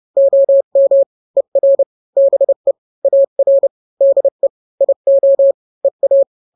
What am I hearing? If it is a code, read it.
Morse code, heard as a keyed tone, about 30 wpm, 550 Hz. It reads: OM ER BE AR DE IO EA
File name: n-grams / 2-grams 30 25 WPM